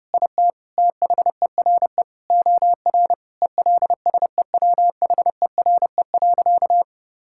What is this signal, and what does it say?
Morse code, heard as a keyed tone, about 32 wpm, 700 Hz. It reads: IT THERE OR ELSEWHERE.
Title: Sentences from Top 1600 Words 30wpm